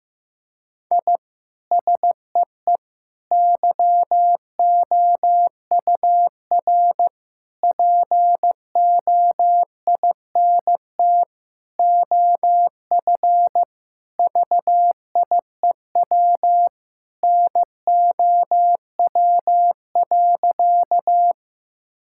Morse code, heard as a keyed tone, about 15 wpm, 700 Hz. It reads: I SEE YOUR POINT OF VIEW NOW.
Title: Sentences from Top 1000 Words - Review Part 6 - 15wpm